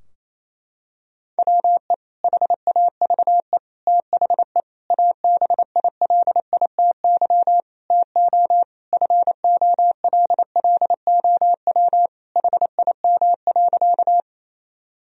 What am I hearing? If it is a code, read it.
Morse code, heard as a keyed tone, about 28 wpm, 700 Hz. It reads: WE HAVE THE ABILITY TO FOLLOW HIM.
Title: Sentences from Top 1000 Words - Review Part 3 - 28wpm